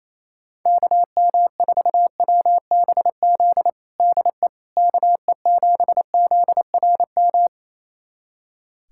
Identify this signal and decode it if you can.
Morse code, heard as a keyed tone, about 28 wpm, 700 Hz. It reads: KM4WBZ DE KE7ZRM
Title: USA Callsign DE Callsign 28wpm